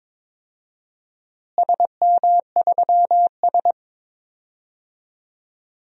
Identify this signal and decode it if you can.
Morse code, heard as a keyed tone, about 22 wpm, 700 Hz. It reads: SM3S
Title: DXCC-100 Call Signs - 4 Characters - Repeated 3x - 22wpm